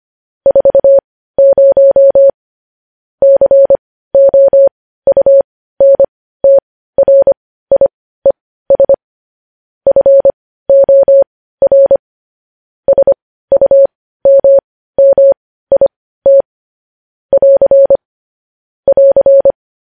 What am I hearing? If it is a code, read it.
Morse code, heard as a keyed tone, about 25 wpm, spaced as for 15 wpm, 550 Hz. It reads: 40 COUNTRIES FOR SUMMIT <AR> <AR>
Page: news-headlines-in-morse-code-at-15-wpm